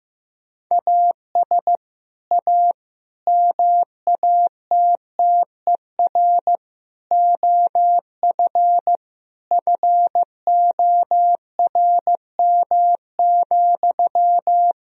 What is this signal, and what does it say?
Morse code, heard as a keyed tone, about 15 wpm, 700 Hz. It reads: AS A MATTER OF FORM,
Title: Sentences from Top 1000 Words - Review Part 1 - 15wpm